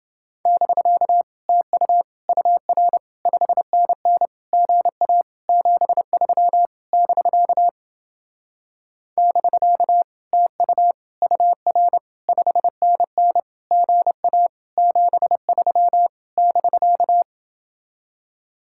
Morse code, heard as a keyed tone, about 29 wpm, 700 Hz. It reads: <BK> TU UR 5NN GA 73 <BK> <BK> TU UR 5NN GA 73 <BK>